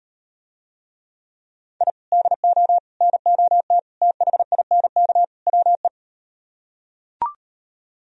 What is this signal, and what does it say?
Morse code, heard as a keyed tone, about 39 wpm, 700 Hz. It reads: I DO NOT THINK WE